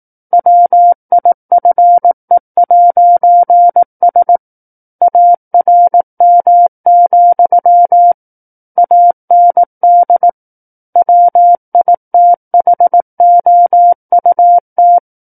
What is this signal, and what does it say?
Morse code, heard as a keyed tone, about 18 wpm, 700 Hz. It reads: WIFE'S ARM, AND WITHOUT
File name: 18 WPM